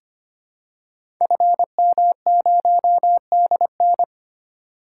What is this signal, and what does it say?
Morse code, heard as a keyed tone, about 25 wpm, 700 Hz. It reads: FM0DN
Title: Code Groups - 5 Letter:Numbers - 25wpm